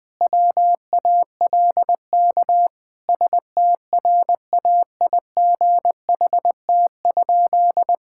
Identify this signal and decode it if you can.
Morse code, heard as a keyed tone, about 20 wpm, 700 Hz. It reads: WALK STRAIGHT?
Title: Sentences from Top 1000 Words - Review Part 6 - 20wpm